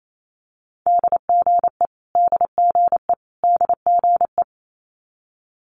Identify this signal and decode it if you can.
Morse code, heard as a keyed tone, about 28 wpm, 700 Hz. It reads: DGE DGE DGE
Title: Multi-letter Phonemes Repeated 3x 28wpm